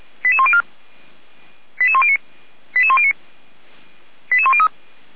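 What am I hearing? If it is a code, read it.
Five-tone selective call (ZVEI), heard as quick runs of five tones, tones of about 70 ms. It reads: 88104 88109 88109 88103
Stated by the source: ZVEI five-tone selective call received on SDR.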